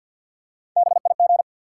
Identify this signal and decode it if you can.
Morse code, heard as a keyed tone, about 52 wpm, 700 Hz. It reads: BIG
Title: Three Letter Words - 50wpm